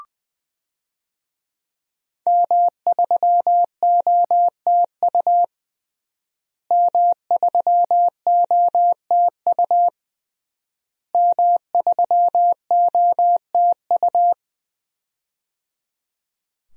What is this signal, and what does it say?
Morse code, heard as a keyed tone, about 20 wpm, 700 Hz. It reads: M3OTU M3OTU M3OTU
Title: DXCC-100 Call Signs - 5 Characters - Repeated 3x - 20wpm